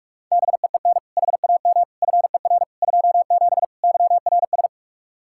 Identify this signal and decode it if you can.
Morse code, heard as a keyed tone, about 41 wpm, 700 Hz. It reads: BEEN HAM FER 27 YRS
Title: Practice QSO - Exchange 3 - More Info - 45wpm